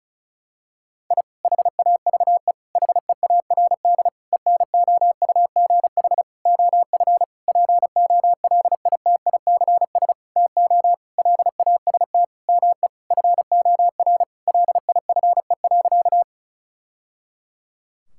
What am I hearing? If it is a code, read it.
Morse code, heard as a keyed tone, about 36 wpm, 700 Hz. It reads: I HAVE HEARD ENOUGH OF POLITICS TO LAST ME FOR LIFE.
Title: Sentences from Top 1000 Words - Review Part 6 - 35wpm